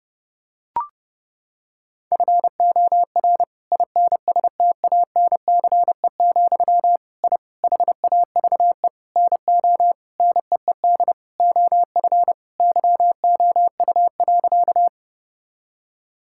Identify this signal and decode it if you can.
Morse code, heard as a keyed tone, about 30 wpm, 700 Hz. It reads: FOR INSTANCE, I HAVE NO NEED OF YOU.
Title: Sentences from Top 1000 Words - Review Part 5 - 30wpm